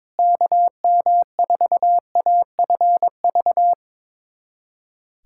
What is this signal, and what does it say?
Morse code, heard as a keyed tone, about 22 wpm, 700 Hz. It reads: KM4AFV
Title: USA Callsign DE Callsign - 6 Characters - 22wpm